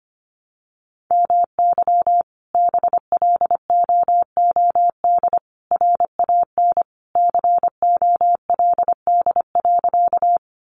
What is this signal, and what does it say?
Morse code, heard as a keyed tone, about 25 wpm, 700 Hz. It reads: MY BLOOD RAN COLD.
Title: Sentences from Top 1100 Words 25wpm